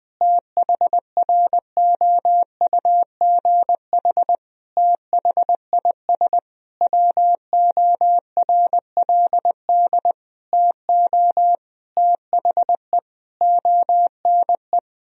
Morse code, heard as a keyed tone, about 20 wpm, 700 Hz. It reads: THROUGH THIS WORLD TO THE ONE
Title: Sentences from Top 1000 Words - Review Part 2 - 20wpm